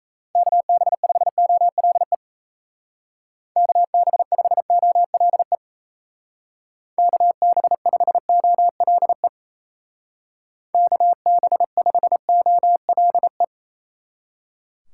Morse code, heard as a keyed tone, about 36 wpm, 700 Hz. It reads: KB5OLE KB5OLE KB5OLE KB5OLE